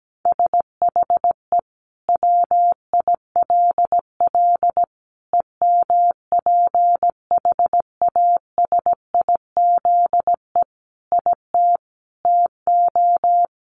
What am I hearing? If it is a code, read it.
Morse code, heard as a keyed tone, about 17 wpm, 700 Hz. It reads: SHE WILL EMPHASIZE IT TO
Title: Sentences from Top 1500 Words 17wpm